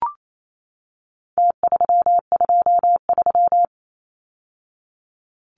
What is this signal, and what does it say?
Morse code, heard as a keyed tone, about 28 wpm, 700 Hz. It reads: T323
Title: Numbers - Contest Style - 4-Digit 28wpm